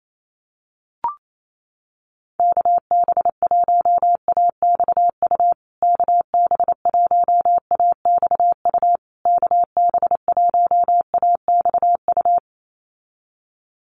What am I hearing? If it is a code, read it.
Morse code, heard as a keyed tone, about 28 wpm, 700 Hz. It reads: KB1AXU KB1AXU KB1AXU
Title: USA Call Signs Repeated 3x 28wpm